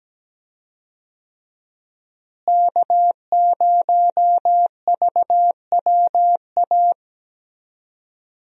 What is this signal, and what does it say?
Morse code, heard as a keyed tone, about 17 wpm, 700 Hz. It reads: K0VWA
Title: USA Call Signs - 5 Characters - Repeated 3x - 17wpm